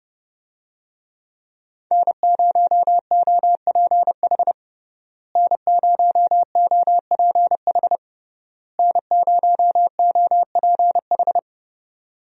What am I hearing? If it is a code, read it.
Morse code, heard as a keyed tone, about 32 wpm, 700 Hz. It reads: N0OPH N0OPH N0OPH
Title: USA Call Signs - 5 Characters - Repeated 3x - 30wpm